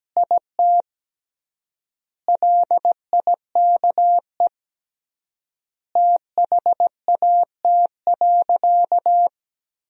Morse code, heard as a keyed tone, about 17 wpm, 700 Hz. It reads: IT LIKE THAT.
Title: 3x Word-Spacing - Sentences from Top 500 Words - 17wpm